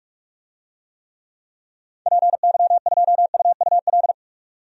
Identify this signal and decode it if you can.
Morse code, heard as a keyed tone, about 46 wpm, 700 Hz. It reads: PY2UUL
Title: DXCC-100 Call Signs - 6 Characters - 45wpm